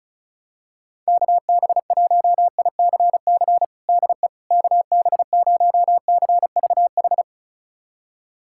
Morse code, heard as a keyed tone, about 36 wpm, 700 Hz. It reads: KB1ICC DE KB0CVH
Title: USA Callsign DE Callsign 35wpm